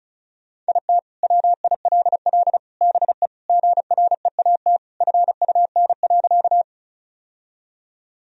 Morse code, heard as a keyed tone, about 36 wpm, 700 Hz. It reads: IT WILL BE GREAT FUN.
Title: Sentences from Top 1000 Words - Review Part 4 - 35wpm